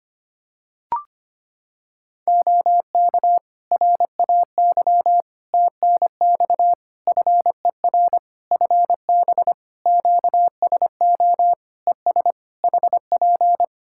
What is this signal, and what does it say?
Morse code, heard as a keyed tone, about 25 wpm, 700 Hz. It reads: OK RAY TNX FER FB QSO ES HP
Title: Practice QSO - Exchange 4 - Ending - 25wpm